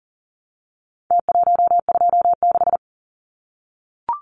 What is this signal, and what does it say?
Morse code, heard as a keyed tone, about 40 wpm, 700 Hz. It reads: T126
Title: Numbers - Contest Style - 4-Digit 40wpm